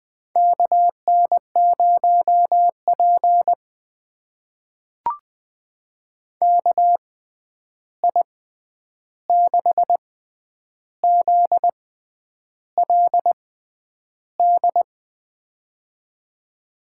Morse code, heard as a keyed tone, about 20 wpm, spaced as for 9 wpm, 700 Hz. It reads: KN0P KI6ZLD
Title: USA Call Signs - 6x Character Spacing - 20wpm